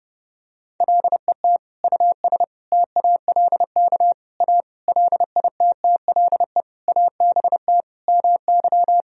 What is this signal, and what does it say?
Morse code, heard as a keyed tone, about 30 wpm, 700 Hz. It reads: LET US TALK A LITTLE ABT MY
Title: Sentences from Top 400 Words and Abbreviations 30wpm